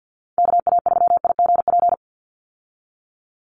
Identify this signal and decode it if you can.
Morse code, heard as a keyed tone, about 50 wpm, 700 Hz. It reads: KA3IGP
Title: USA Call Signs 50wpm